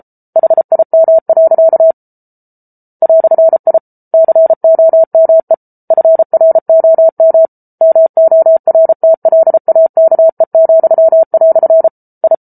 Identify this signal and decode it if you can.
Morse code, heard as a keyed tone, about 34 wpm, 650 Hz. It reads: HIM. "I COME FROM MORTLAKE," I